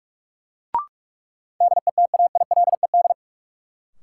Morse code, heard as a keyed tone, about 44 wpm, 700 Hz. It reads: DETAILED
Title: Sentences from Top 1500 Words - New Word Drill - 45wpm